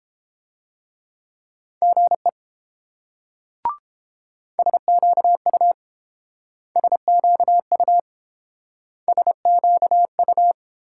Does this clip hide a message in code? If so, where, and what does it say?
Yes: Morse code from 1.8 to 10.5 s, heard as a keyed tone, about 30 wpm, 690 Hz. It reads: GE SQU SQU SQU